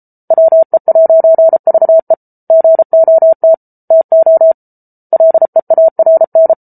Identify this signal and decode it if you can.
Morse code, heard as a keyed tone, about 34 wpm, 650 Hz. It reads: WE'VE GOT TO LEARN